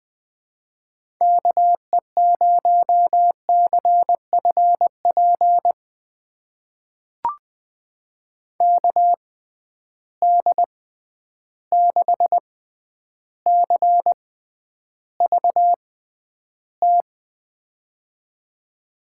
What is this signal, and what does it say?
Morse code, heard as a keyed tone, about 20 wpm, spaced as for 10 wpm, 700 Hz. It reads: KE0CFP KD6CVT
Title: USA Call Signs - 6 Characters - 6x Character Spacing - 20wpm